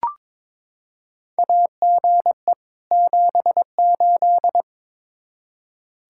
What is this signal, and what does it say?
Morse code, heard as a keyed tone, about 22 wpm, 700 Hz. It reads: AGE 78
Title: K1USN SST QSO Practice 22wpm